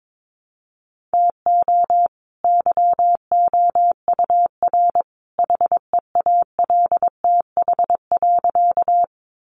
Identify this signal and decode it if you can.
Morse code, heard as a keyed tone, about 22 wpm, 700 Hz. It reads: TO YOUR HEALTH.
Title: Sentences from Top 600 Words 22wpm